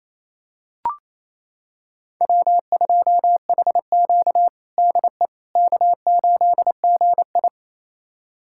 Morse code, heard as a keyed tone, about 28 wpm, 700 Hz. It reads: W2HQ DE K8GI
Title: USA Callsign DE Callsign - 4 Characters - 28wpm